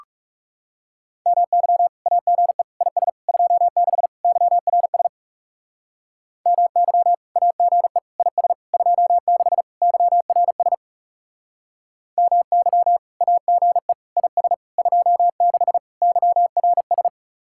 Morse code, heard as a keyed tone, about 37 wpm, 700 Hz. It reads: MY AGE IS 26 YRS MY AGE IS 26 YRS MY AGE IS 26 YRS
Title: Speed-Racing Common QSO Phrases - 30wpm